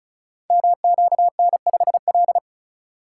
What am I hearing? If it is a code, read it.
Morse code, heard as a keyed tone, about 36 wpm, 700 Hz. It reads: MQN5L
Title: Code Groups - 5 Letter:Numbers - 35wpm